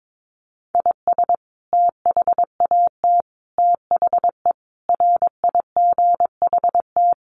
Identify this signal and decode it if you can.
Morse code, heard as a keyed tone, about 22 wpm, 700 Hz. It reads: IS THAT THE RIGHT
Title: Sentences from Top 1200 Words 22wpm